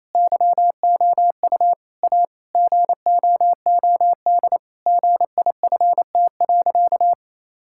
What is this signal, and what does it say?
Morse code, heard as a keyed tone, about 28 wpm, 700 Hz. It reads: YOU A GOOD GIFT.
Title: Sentences from Top 1000 Words - Review Part 6 - 28wpm